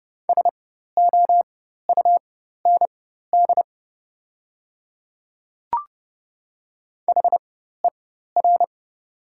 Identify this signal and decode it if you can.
Morse code, heard as a keyed tone, about 30 wpm, spaced as for 14 wpm, 700 Hz. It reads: SOUND HER
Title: 4x Character Spacing - Top 100 Words - 30wpm